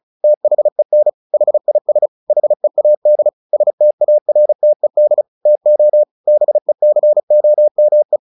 Morse code, heard as a keyed tone, about 34 wpm, 600 Hz. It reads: THEN HIS HEAD STARTED TO BECOME
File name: the old man and the sea 2 35wpm